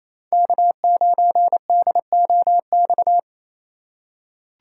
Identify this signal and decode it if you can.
Morse code, heard as a keyed tone, about 28 wpm, 700 Hz. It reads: K9DOX
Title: USA Call Signs - 5 Characters - 2x Character Spacing - 28wpm